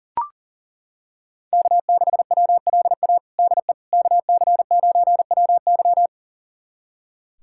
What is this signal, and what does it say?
Morse code, heard as a keyed tone, about 37 wpm, 700 Hz. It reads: K6WLA DE KC9WY
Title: USA Callsign DE Callsign 40wpm